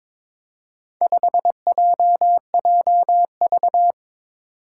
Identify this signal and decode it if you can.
Morse code, heard as a keyed tone, about 22 wpm, 700 Hz. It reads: 5JJV
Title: Four Challenging Letter-Number Combinations 22wpm